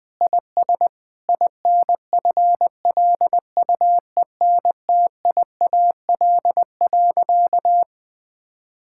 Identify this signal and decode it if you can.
Morse code, heard as a keyed tone, about 20 wpm, 700 Hz. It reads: IS INFLUENTIAL.
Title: Sentences with 4-Word Phrases - 20wpm